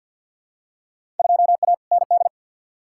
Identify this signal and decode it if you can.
Morse code, heard as a keyed tone, about 53 wpm, 700 Hz. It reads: 2A ND
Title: Field Day Exchanges Encore - 50wpm